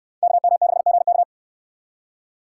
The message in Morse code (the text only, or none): KM6GK